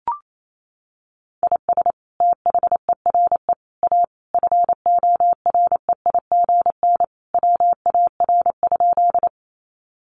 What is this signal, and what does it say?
Morse code, heard as a keyed tone, about 28 wpm, 700 Hz. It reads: IS THERE A FOREIGN WAR?
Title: Sentences from Top 1400 Words 28wpm